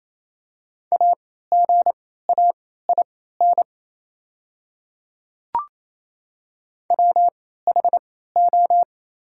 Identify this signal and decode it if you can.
Morse code, heard as a keyed tone, about 28 wpm, spaced as for 16 wpm, 700 Hz. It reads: AGAIN WHO
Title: 3x Character Spacing - Top 300 Words - 28wpm